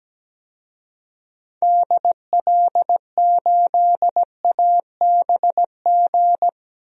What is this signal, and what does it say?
Morse code, heard as a keyed tone, about 17 wpm, 700 Hz. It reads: DL8ABG
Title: DXCC-100 Call Signs - 6 Characters - 17wpm